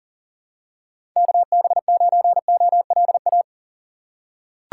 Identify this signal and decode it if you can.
Morse code, heard as a keyed tone, about 40 wpm, 700 Hz. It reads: KB9OLA
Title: USA Call Signs - 6 Characters - 4x Character Spacing - 40wpm